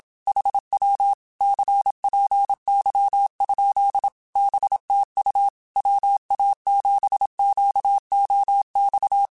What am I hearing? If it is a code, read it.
Morse code, heard as a keyed tone, about 27 wpm, 800 Hz. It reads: HW CPY? BTU WA7QOX